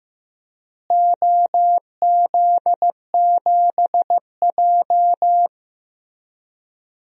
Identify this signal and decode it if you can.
Morse code, heard as a keyed tone, about 16 wpm, 700 Hz. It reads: OZ7J
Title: DXCC-100 Call Signs - 4 Characters - Repeated 2x - 15wpm